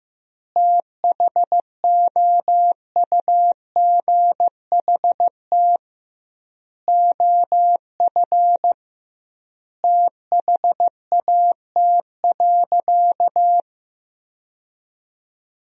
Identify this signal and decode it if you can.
Morse code, heard as a keyed tone, about 15 wpm, 700 Hz. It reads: THOUGHT OF THAT.
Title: 2x Word-Spacing - Sentences from Top 300 Words - 15wpm